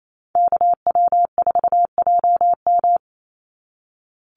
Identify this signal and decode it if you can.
Morse code, heard as a keyed tone, about 28 wpm, 700 Hz. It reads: KW4JM
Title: NA SOTA Activator Callsigns 28wpm